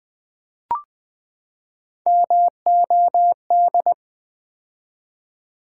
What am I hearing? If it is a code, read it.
Morse code, heard as a keyed tone, about 20 wpm, 700 Hz. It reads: MOD
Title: Build A Word - 3 Letters At A Time - 8-Letter Words - 20wpm